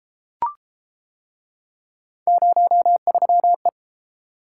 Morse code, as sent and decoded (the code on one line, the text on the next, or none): ----- ...-- .
03E